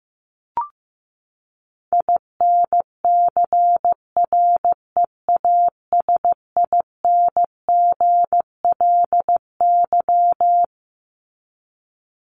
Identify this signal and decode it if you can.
Morse code, heard as a keyed tone, about 15 wpm, 700 Hz. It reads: INCREASINGLY
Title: LY Suffix - Top 100 Words with Base Words 15wpm